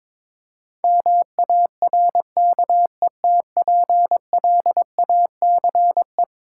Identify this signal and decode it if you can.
Morse code, heard as a keyed tone, about 22 wpm, 700 Hz. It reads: MARKETPLACE
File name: practice / 22wpm / Eleven Letter Words - 22wpm